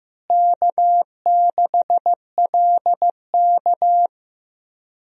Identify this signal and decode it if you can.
Morse code, heard as a keyed tone, about 15 wpm, 700 Hz. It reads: K6LK
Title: USA Call Signs - 4 Characters - Repeated 3x - 15wpm